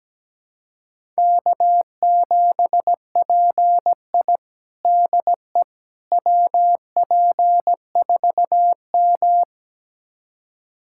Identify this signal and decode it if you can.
Morse code, heard as a keyed tone, about 17 wpm, 700 Hz. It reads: K7PI DE WP4M
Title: USA Callsign DE Callsign - 4 Characters - 17wpm